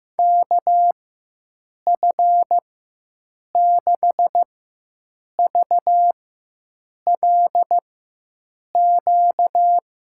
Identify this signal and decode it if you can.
Morse code, heard as a keyed tone, about 15 wpm, spaced as for 7 wpm, 700 Hz. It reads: KF6VLQ
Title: USA Call Signs - 6 Characters - 4x Character Spacing - 15wpm